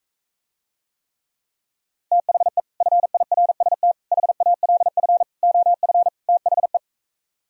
Morse code, heard as a keyed tone, about 45 wpm, 700 Hz. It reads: THE FIRST HALF OF THE